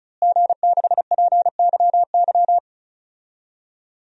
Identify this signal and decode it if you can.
Morse code, heard as a keyed tone, about 36 wpm, 700 Hz. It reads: G6PYY
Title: DXCC-100 Call Signs - 5 Characters - Repeated 3x - 35wpm